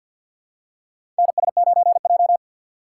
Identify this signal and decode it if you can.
Morse code, heard as a keyed tone, about 52 wpm, 700 Hz. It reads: NS9J
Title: USA Call Signs - 4x Character Spacing - 50wpm